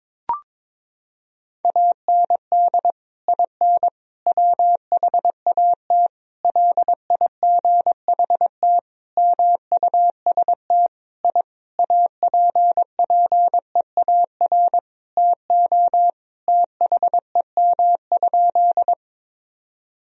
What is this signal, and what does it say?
Morse code, heard as a keyed tone, about 22 wpm, 700 Hz. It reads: AND IN WHAT LIGHT MUST I APPEAR TO THEM?
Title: Sentences from Top 1000 Words - Review Part 2 - 22wpm